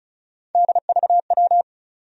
Morse code, heard as a keyed tone, about 36 wpm, 700 Hz. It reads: DVW
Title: USA Call Sign Postfixes - 3 Letters 35wpm